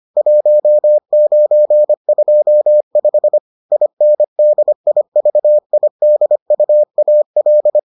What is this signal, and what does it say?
Morse code, heard as a keyed tone, about 25 wpm, 600 Hz. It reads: 1925 INDIVIDUAL